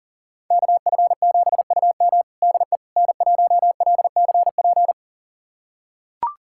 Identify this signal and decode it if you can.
Morse code, heard as a keyed tone, about 40 wpm, 700 Hz. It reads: KF7UM DE N1LCP
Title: USA Callsign DE Callsign - 5 Characters - 40wpm